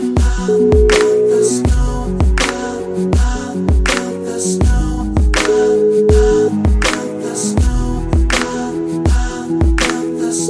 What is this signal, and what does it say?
Ringback tone (Europe, 425 Hz), with noise.